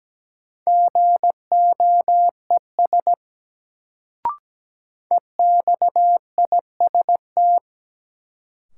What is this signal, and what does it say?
Morse code, heard as a keyed tone, about 17 wpm, 700 Hz. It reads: GOES EXIST
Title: Sentences from Top 1000 Words - Review Part 5 - New Word Drill - 17wpm